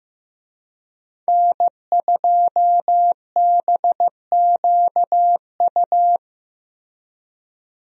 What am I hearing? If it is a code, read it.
Morse code, heard as a keyed tone, about 15 wpm, 700 Hz. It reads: N2BQU
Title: USA Call Signs - 5 Characters - Repeated 3x - 15wpm